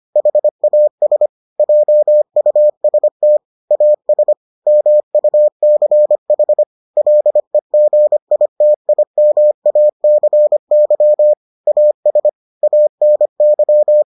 Morse code, heard as a keyed tone, about 25 wpm, 600 Hz. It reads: HAS JUST AS MUCH LEGITIMACY AS ANY